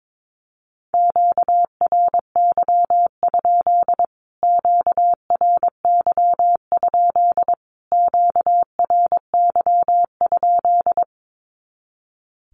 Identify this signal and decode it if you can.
Morse code, heard as a keyed tone, about 22 wpm, 700 Hz. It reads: QRY? QRY? QRY?
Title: Q-Codes Repeated 3x 22wpm